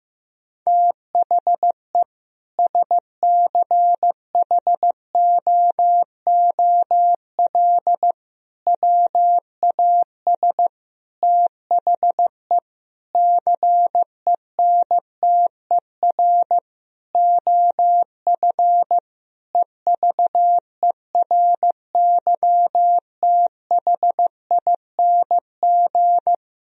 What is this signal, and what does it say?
Morse code, heard as a keyed tone, about 16 wpm, 700 Hz. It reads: THE SCHOOL WAS THE CENTER OF EVERYTHING
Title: Sentences from Top 1000 Words - Review Part 2 - 15wpm